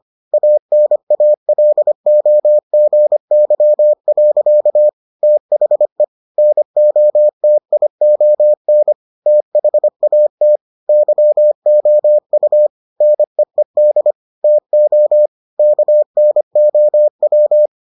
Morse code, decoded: ANALOGY. THE NOTION THAT YOU NEED TO KNOW

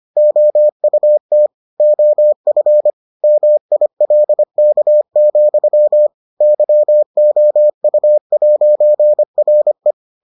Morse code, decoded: OUT OF MILK, YOU'RE

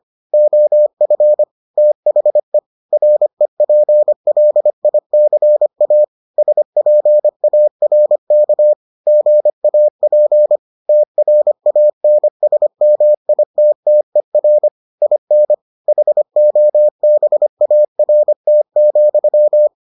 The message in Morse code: OF THE REPLICA SPARK GAP TRANSMITTER IN HOBART,

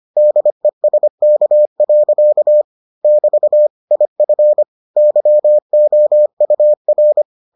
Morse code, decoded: DESK. <BT> IF YOUR

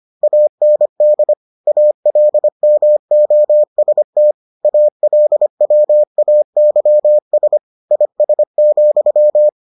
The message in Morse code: AND ALMOST ALWAYS IS,

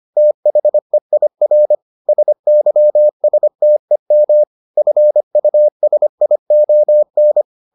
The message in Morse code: THEIR SYSTEM FUSION

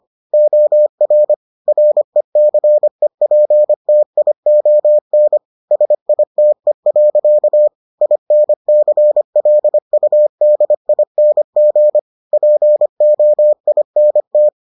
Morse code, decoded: OR RECEPTION SITE. INCLUDING POINT

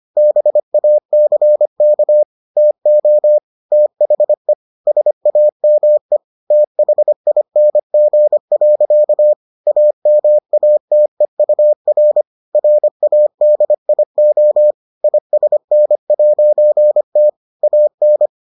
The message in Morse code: BACK TO THE SAME THING. AMATEUR RADIO ISN'T AN

